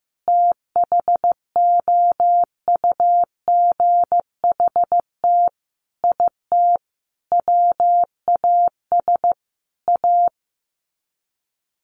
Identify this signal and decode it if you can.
Morse code, heard as a keyed tone, about 15 wpm, 700 Hz. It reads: THOUGHT IT WAS A
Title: Sets of 5 Words 15wpm